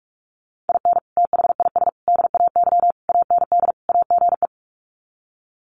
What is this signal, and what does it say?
Morse code, heard as a keyed tone, about 44 wpm, 700 Hz. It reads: IN THIS DAY AND AGE
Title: Sets of 5 Words Encore 45wpm